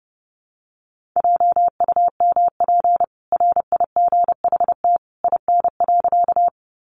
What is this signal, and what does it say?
Morse code, heard as a keyed tone, about 30 wpm, 700 Hz. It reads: JUMP RIGHT IN.
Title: Sentences from Top 1300 Words 30wpm